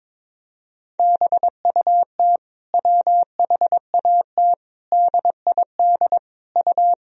Morse code, heard as a keyed tone, about 21 wpm, 700 Hz. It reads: BUT WHAT DID U
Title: Sentences from Top 100 Words and Abbreviations 22wpm